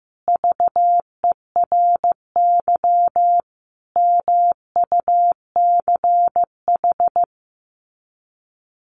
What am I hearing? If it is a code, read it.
Morse code, heard as a keyed tone, about 15 wpm, 700 Hz. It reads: VERY MUCH
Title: Sets of 5 Words 15wpm